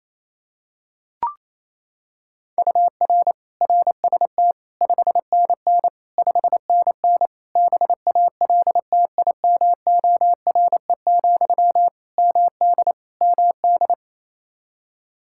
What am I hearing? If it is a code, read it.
Morse code, heard as a keyed tone, about 28 wpm, 700 Hz. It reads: UR RST 5NN 5NN BALTIMORE, MD MD
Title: Practice QSO - Exchange 1 - Beginning - 28wpm